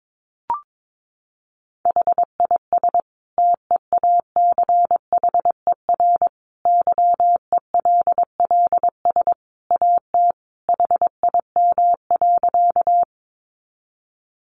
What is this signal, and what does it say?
Morse code, heard as a keyed tone, about 22 wpm, 700 Hz. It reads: HIS TEACHER YELLS AT HIM.
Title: Additional Easy Phrases 22wpm